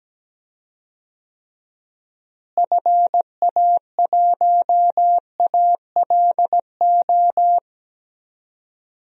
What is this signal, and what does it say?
Morse code, heard as a keyed tone, about 17 wpm, 700 Hz. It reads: FA1ALO
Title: DXCC-100 Call Signs - 6 Characters - Repeated 2x - 17wpm